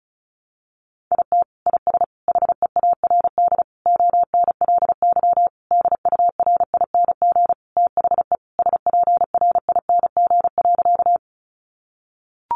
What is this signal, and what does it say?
Morse code, heard as a keyed tone, about 34 wpm, 700 Hz. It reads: IT IS HEARD ONLY DURING THE SPRING.
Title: Sentences from Top 1200 Words 35wpm